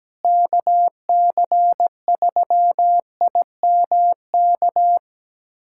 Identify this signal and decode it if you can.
Morse code, heard as a keyed tone, about 17 wpm, 700 Hz. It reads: KC3IMK
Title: USA Callsign DE Callsign - 6 Characters - 17wpm